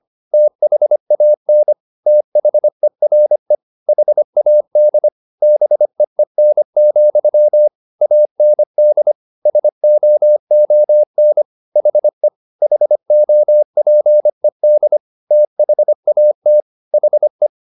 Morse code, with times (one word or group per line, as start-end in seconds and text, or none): 0.3-1.7 THAN
2.1-3.5 THERE
3.9-5.1 HAD
5.4-7.7 BEEN,
8.0-9.1 AND
9.5-11.4 SOON
11.8-12.3 HE
12.6-15.0 HOPED
15.3-16.6 THAT
16.9-17.5 HE